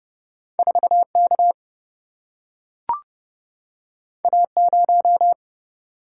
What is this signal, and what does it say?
Morse code, heard as a keyed tone, about 30 wpm, 700 Hz. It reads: 4 K A 0
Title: Single-Letter Single-Number 30wpm